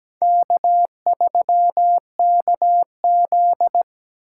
Morse code, heard as a keyed tone, about 20 wpm, 700 Hz. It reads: K3KZ
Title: Speed Racing - USA Call Signs 17wpm